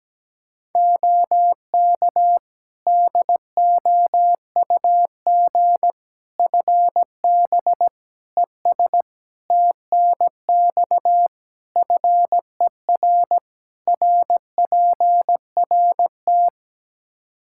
Morse code, heard as a keyed tone, about 17 wpm, 700 Hz. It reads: OK DOUG FB ES TNX FER RPRT
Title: K1USN SST QSO Practice 17wpm